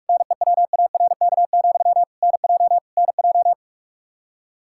Morse code, heard as a keyed tone, about 46 wpm, 700 Hz. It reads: NEWARK, NJ NJ
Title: K1USN SST QSO Practice 45wpm